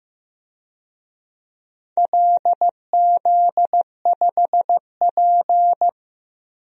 Morse code, heard as a keyed tone, about 15 wpm, 700 Hz. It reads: LZ5P
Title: DXCC-100 Call Signs - 4 Characters - Repeated 2x - 15wpm